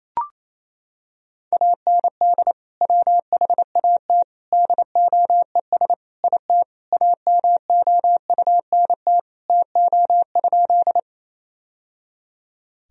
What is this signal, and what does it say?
Morse code, heard as a keyed tone, about 28 wpm, 700 Hz. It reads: AND WHAT DOES IT AMOUNT TO?
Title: Sentences from Top 1000 Words - Review Part 4 - 28wpm